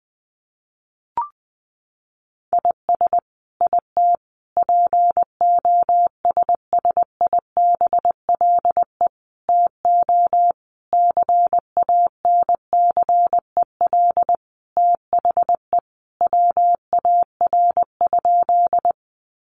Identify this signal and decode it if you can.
Morse code, heard as a keyed tone, about 20 wpm, 700 Hz. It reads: IS IT POSSIBLE TO CANCEL THE WAR?
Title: Sentences from Top 1200 Words 20wpm